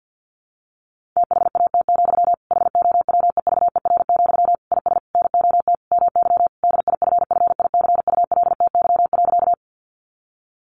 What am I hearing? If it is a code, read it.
Morse code, heard as a keyed tone, about 50 wpm, 700 Hz. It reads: THAT, HOWEVER, IS NOT MY DIFFICULTY.
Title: Sentences from Top 1500 Words 50wpm